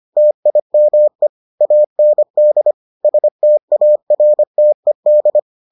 Morse code, heard as a keyed tone, about 25 wpm, 600 Hz. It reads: TIME AND STARTED